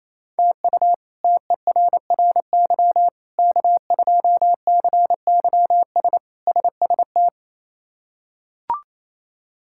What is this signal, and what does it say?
Morse code, heard as a keyed tone, about 28 wpm, 700 Hz. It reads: TU TERRY K2CYS SST
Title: K1USN SST QSO Practice - 28wpm